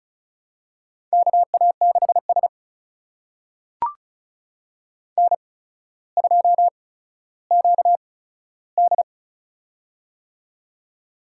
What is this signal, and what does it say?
Morse code, heard as a keyed tone, about 35 wpm, spaced as for 15 wpm, 700 Hz. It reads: KA6S N2QD
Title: USA Call Signs - 4 Characters - 8x Character Spacing - 35wpm